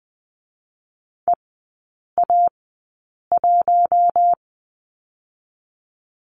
Morse code, heard as a keyed tone, about 20 wpm, 700 Hz. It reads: E A 1
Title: 121 Lesson 20e - Number 3 - Three Characters 20wpm